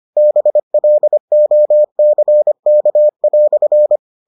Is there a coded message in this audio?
Yes: Morse code from 0.2 to 4.0 s, heard as a keyed tone, about 25 wpm, 600 Hz. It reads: BLOCK"